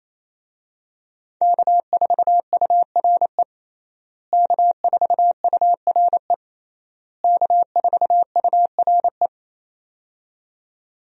Morse code, heard as a keyed tone, about 28 wpm, 700 Hz. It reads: K4URE K4URE K4URE
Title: USA Call Signs - 5 Characters - Repeated 3x - 28wpm